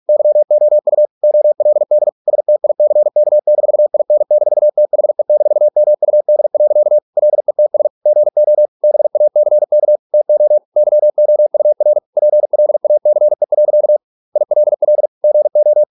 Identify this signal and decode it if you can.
Morse code, heard as a keyed tone, about 46 wpm, 600 Hz. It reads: YOU OLD STICK-IN-THE-MUD. LETS GO BACK TO YOUR PLACE. ILL GO